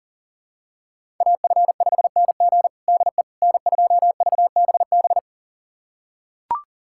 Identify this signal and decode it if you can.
Morse code, heard as a keyed tone, about 37 wpm, 700 Hz. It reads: AF5NG DE N2VBB
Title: USA Callsign DE Callsign - 5 Characters - 40wpm